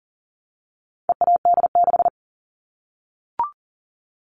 Morse code, heard as a keyed tone, about 40 wpm, 700 Hz. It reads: EAD6
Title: All Cut Numbers - 4-Digit 40wpm